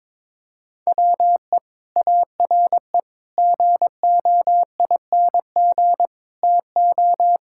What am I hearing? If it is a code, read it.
Morse code, heard as a keyed tone, about 21 wpm, 700 Hz. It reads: WE ARE GOING TO
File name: practice / 22wpm / Sets of 6 Words - 22wpm